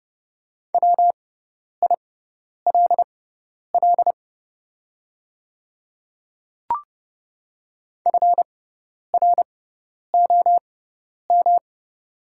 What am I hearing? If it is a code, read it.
Morse code, heard as a keyed tone, about 30 wpm, spaced as for 10 wpm, 690 Hz. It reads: WILL FROM
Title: 6x Character Spacing - Top 100 Words - 30wpm